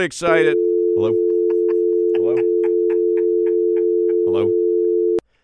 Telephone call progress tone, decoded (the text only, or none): dial tone